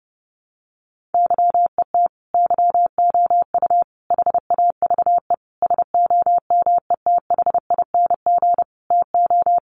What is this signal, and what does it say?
Morse code, heard as a keyed tone, about 30 wpm, 700 Hz. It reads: YET YOU HAVE SOMETHING TO
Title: Sentences from Top 600 Words 30wpm